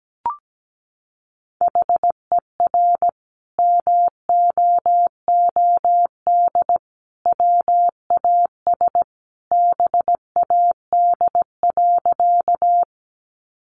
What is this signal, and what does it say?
Morse code, heard as a keyed tone, about 17 wpm, 700 Hz. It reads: HER MOOD WAS BAD.
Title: Sentences from Top 1100 Words 17wpm